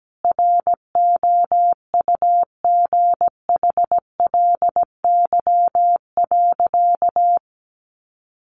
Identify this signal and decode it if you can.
Morse code, heard as a keyed tone, about 17 wpm, 700 Hz. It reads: ROUGHLY.
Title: Sentences from Top 1500 Words 17wpm